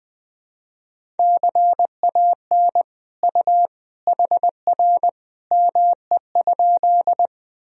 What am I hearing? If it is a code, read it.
Morse code, heard as a keyed tone, about 20 wpm, 700 Hz. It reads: CAN U HR ME?
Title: Sentences from Top 300 Words and Abbreviations 20wpm